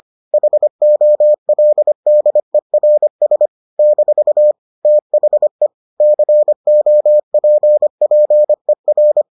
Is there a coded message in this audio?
Yes: Morse code from 0.3 to 9.2 s, heard as a keyed tone, about 25 wpm, 600 Hz. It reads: HOLDERS - THE COPPER